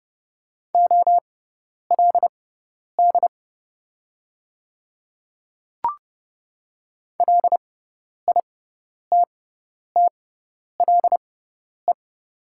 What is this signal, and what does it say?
Morse code, heard as a keyed tone, about 30 wpm, spaced as for 10 wpm, 700 Hz. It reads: OLD LITTLE
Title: 6x Character Spacing - Top 200 Words - 30wpm